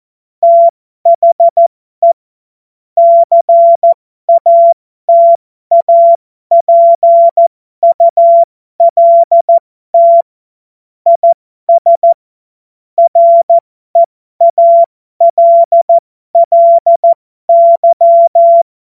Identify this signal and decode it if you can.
Morse code, heard as a keyed tone, about 14 wpm, spaced as for 12 wpm, 680 Hz. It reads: THE CATAPULT IS REALLY
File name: Morsekurs / Witze